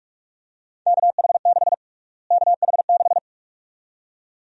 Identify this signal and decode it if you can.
Morse code, heard as a keyed tone, about 42 wpm, 700 Hz. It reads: KH6 KH6
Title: USA Call Sign Prefixes - Repeated 2x - 45wpm